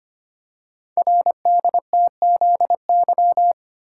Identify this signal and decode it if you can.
Morse code, heard as a keyed tone, about 25 wpm, 700 Hz. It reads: RDTZY
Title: Code Groups - 5 Letters - 25wpm